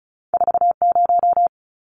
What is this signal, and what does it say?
Morse code, heard as a keyed tone, about 36 wpm, 700 Hz. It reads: 40
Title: Numbers - 2-Digit 35wpm